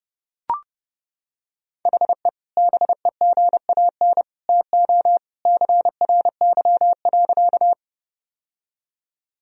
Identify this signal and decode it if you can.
Morse code, heard as a keyed tone, about 29 wpm, 700 Hz. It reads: HE BEGAN TO CRY.
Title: Sentences from Top 1000 Words - Review Part 2 - 30wpm